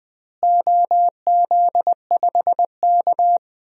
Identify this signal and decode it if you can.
Morse code, heard as a keyed tone, about 21 wpm, 700 Hz. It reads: OZ5K
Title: DXCC-100 Call Signs - 4 Characters - Repeated 2x - 20wpm